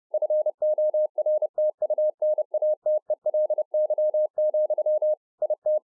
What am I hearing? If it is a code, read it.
Morse code, heard as a keyed tone, about 30 wpm, 610 Hz. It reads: FORTUNATELY, IT